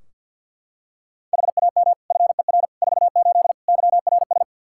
Morse code, heard as a keyed tone, about 46 wpm, 700 Hz. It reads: HAM FER 48 YRS
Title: K1USN SST QSO Practice 50wpm